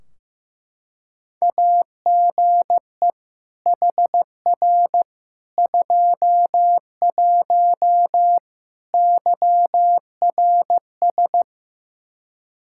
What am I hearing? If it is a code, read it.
Morse code, heard as a keyed tone, about 15 wpm, 700 Hz. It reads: AGE HR 21 YRS
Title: K1USN SST QSO Practice 15wpm